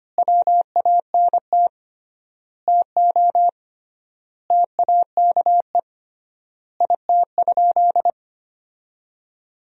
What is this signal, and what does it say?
Morse code, heard as a keyed tone, about 25 wpm, 700 Hz. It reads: WANT TO TAKE IT?
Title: 3x Word-Spacing - Sentences from Top 300 Words - 25wpm